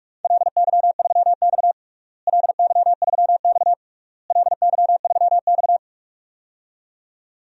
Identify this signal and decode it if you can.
Morse code, heard as a keyed tone, about 46 wpm, 700 Hz. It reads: LY3X LY3X LY3X
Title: DXCC-100 Call Signs - 4 Characters - Repeated 3x - 45wpm